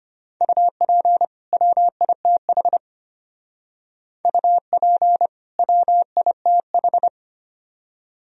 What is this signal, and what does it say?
Morse code, heard as a keyed tone, about 33 wpm, 700 Hz. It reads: UP WITH UP WITH